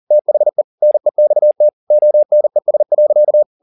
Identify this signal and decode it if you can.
Morse code, heard as a keyed tone, about 40 wpm, 600 Hz. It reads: THE NEXT ONES.
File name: the old man and the sea 2 40wpm